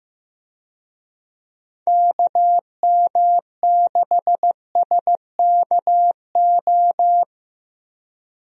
Morse code, heard as a keyed tone, about 15 wpm, 700 Hz. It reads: KM6SKO
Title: USA Call Signs - 6 Characters - 15wpm